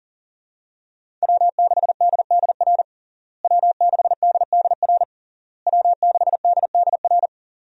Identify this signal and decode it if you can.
Morse code, heard as a keyed tone, about 37 wpm, 700 Hz. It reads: W6DDR W6DDR W6DDR
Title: USA Call Signs - 5 Characters - Repeated 3x - 40wpm